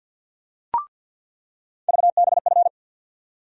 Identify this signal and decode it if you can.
Morse code, heard as a keyed tone, about 48 wpm, 700 Hz. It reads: VBF
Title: USA Call Sign Postfixes - 3 Letters 50wpm